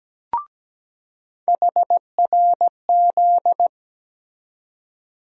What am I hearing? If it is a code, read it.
Morse code, heard as a keyed tone, about 17 wpm, 700 Hz. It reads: HRZ
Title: USA Call Sign Postfixes - 3 Letters 17wpm